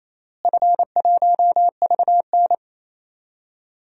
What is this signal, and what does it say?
Morse code, heard as a keyed tone, about 27 wpm, 700 Hz. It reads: F1VN
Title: DXCC-100 Call Signs - 4 Characters - Repeated 3x - 28wpm